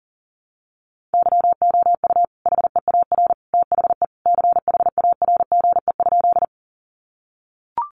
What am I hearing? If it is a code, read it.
Morse code, heard as a keyed tone, about 37 wpm, 700 Hz. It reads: YOU HEAR THE CHARGE?
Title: Sentences from Top 900 Words 40wpm